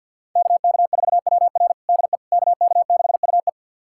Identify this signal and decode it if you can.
Morse code, heard as a keyed tone, about 47 wpm, 700 Hz. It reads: KK4WR DE KK6UE
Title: USA Callsign DE Callsign - 5 Characters - 50wpm